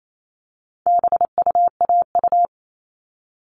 Morse code, heard as a keyed tone, about 28 wpm, 700 Hz. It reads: BUAU
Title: All Cut Numbers - 4-Digit 28wpm